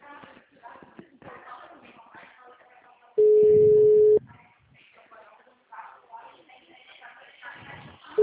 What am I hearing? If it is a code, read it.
Ringback tone (Europe, 425 Hz), with speech.
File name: out- -2002- - - .3204